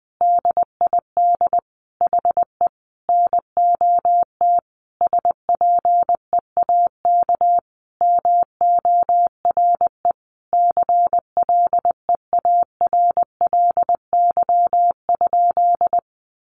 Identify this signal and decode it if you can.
Morse code, heard as a keyed tone, about 20 wpm, 700 Hz. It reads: DID HE NOT SPEAK MORE CLEARLY?
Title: Sentences from Top 1300 Words 20wpm